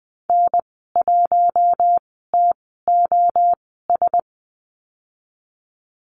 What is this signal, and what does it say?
Morse code, heard as a keyed tone, about 20 wpm, spaced as for 14 wpm, 700 Hz. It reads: N1TOS
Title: 042 Lesson 8g - Number 1 - Call Signs 20wpm